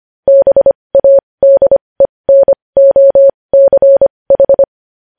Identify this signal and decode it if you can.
Morse code, heard as a keyed tone, about 25 wpm, spaced as for 20 wpm, 550 Hz. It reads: BADENOCH
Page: news-headlines-in-morse-code-at-20-wpm